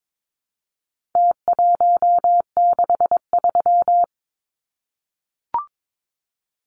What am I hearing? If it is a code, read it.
Morse code, heard as a keyed tone, about 22 wpm, 700 Hz. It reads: T163
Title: Numbers - Contest Style - 4-Digit 22wpm